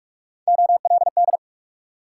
Morse code, heard as a keyed tone, about 60 wpm, spaced as for 53 wpm, 700 Hz. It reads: OLD
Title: Speed-Racing Multi-letter Phonemes - 45wpm